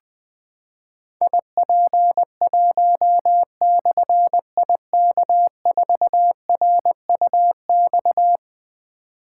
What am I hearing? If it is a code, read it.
Morse code, heard as a keyed tone, about 20 wpm, 700 Hz. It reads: IP1/IK4RUX
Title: Difficult Call Signs 20wpm